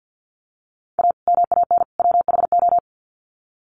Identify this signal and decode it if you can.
Morse code, heard as a keyed tone, about 50 wpm, 700 Hz. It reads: A MAN WHO
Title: Sets of 3 Words 50wpm